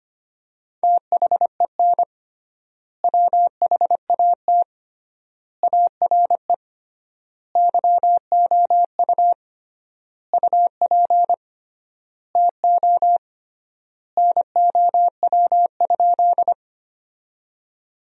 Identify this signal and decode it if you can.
Morse code, heard as a keyed tone, about 25 wpm, 700 Hz. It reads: THEN WHAT ARE YOU UP TO NOW?
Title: 3x Word-Spacing - Sentences from Top 100 Words - 25wpm